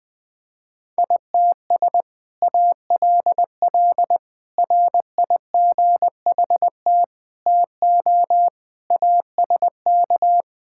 Morse code, heard as a keyed tone, about 20 wpm, 700 Hz. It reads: ITS ALL RIGHT TO ASK
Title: Sentences with 2-Word Phrases - 20wpm